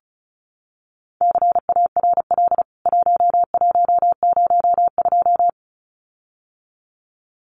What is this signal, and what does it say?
Morse code, heard as a keyed tone, about 34 wpm, 700 Hz. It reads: CARL 1102
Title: CWT Exchange 35wpm